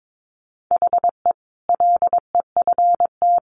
Morse code, heard as a keyed tone, about 22 wpm, 700 Hz. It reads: HE LEFT
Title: Sentences from Top 1100 Words 22wpm